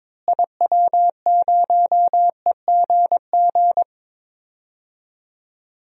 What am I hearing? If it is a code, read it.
Morse code, heard as a keyed tone, about 23 wpm, 700 Hz. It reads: IW0EGG
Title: DXCC-100 Call Signs - 6 Characters - Repeated 2x - 22wpm